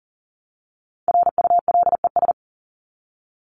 Morse code, heard as a keyed tone, about 40 wpm, 700 Hz. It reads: RULES
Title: ES:S Suffix - Top 100 Words with Base Words 40wpm